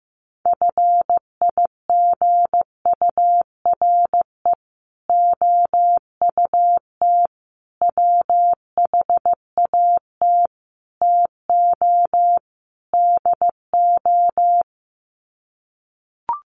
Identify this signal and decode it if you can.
Morse code, heard as a keyed tone, about 14 wpm, 700 Hz. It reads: FIGURE OUT WHAT TO DO
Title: Sets of 5 Words Encore2 15wpm